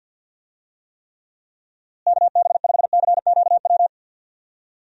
Morse code, heard as a keyed tone, about 52 wpm, 700 Hz. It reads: KB5CQW
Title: USA Call Signs - 6 Characters - Repeated 2x - 50wpm